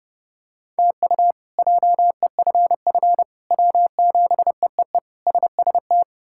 Morse code, heard as a keyed tone, about 31 wpm, 700 Hz. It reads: TU JEFF W7EEE SST
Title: K1USN SST QSO Practice - 30wpm